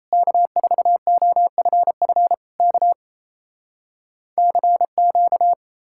Morse code, heard as a keyed tone, about 39 wpm, 700 Hz. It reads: K4OFF K CQ